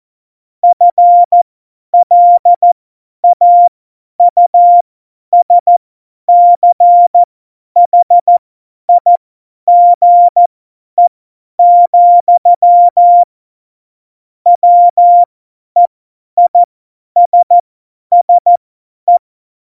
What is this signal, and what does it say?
Morse code, heard as a keyed tone, about 14 wpm, spaced as for 10 wpm, 700 Hz. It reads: FLAUSCHIGE, WEISSE
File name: Morsekurs / Witze / Witz 22. Januar 2019, 50 F70